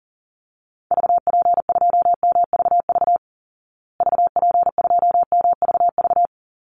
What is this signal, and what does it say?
Morse code, heard as a keyed tone, about 40 wpm, 700 Hz. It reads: VP2MVV VP2MVV
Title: Call Signs Repeated 3x - 40wpm